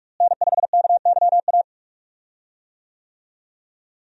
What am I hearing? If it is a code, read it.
Morse code, heard as a keyed tone, about 42 wpm, 700 Hz. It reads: N5KYA
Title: USA Call Signs - 5 Characters - Repeated 2x - 45wpm